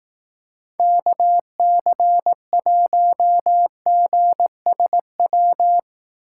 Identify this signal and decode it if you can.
Morse code, heard as a keyed tone, about 18 wpm, 700 Hz. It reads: KC1GSW
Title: Speed Racing - USA Call Signs 15wpm